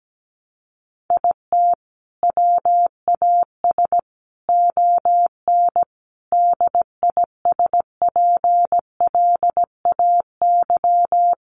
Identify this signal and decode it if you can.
Morse code, heard as a keyed tone, about 17 wpm, 700 Hz. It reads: IT WAS ON DISPLAY
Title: Sentences from Top 1200 Words 17wpm